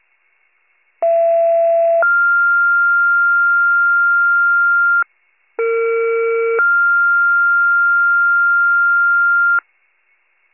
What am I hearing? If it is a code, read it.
Two-tone page, heard as a one-second tone, then a three-second tone, about 670 then 1400 Hz, about 460 then 1400 Hz. It is heard from 1.0 to 9.6 s.